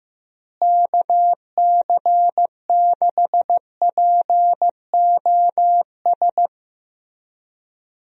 Morse code, heard as a keyed tone, about 16 wpm, 700 Hz. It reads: KC6POS
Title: USA Call Signs - 6 Characters - Repeated 2x - 15wpm